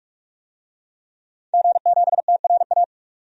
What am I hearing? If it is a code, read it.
Morse code, heard as a keyed tone, about 42 wpm, 700 Hz. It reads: G7TRA
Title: DXCC-100 Call Signs - 5 Characters - 45wpm